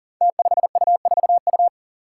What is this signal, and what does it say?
Morse code, heard as a keyed tone, about 38 wpm, 700 Hz. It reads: T5U4U
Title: Five Challenging Letter-Number Combinations 40wpm